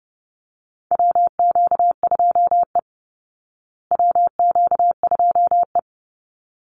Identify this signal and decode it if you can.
Morse code, heard as a keyed tone, about 30 wpm, 700 Hz. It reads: WQ2E WQ2E
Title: Call Signs Repeated 3x - 30wpm